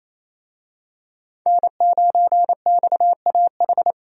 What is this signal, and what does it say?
Morse code, heard as a keyed tone, about 28 wpm, 700 Hz. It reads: N9XAH
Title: USA Call Signs - 5 Characters - Repeated 2x - 28wpm